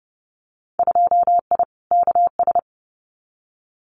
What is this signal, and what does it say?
Morse code, heard as a keyed tone, about 30 wpm, 700 Hz. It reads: 2I KS
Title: Winter Field Day Exchanges Encore 30wpm